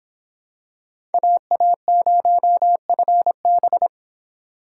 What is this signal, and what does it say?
Morse code, heard as a keyed tone, about 26 wpm, 700 Hz. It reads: AA0FB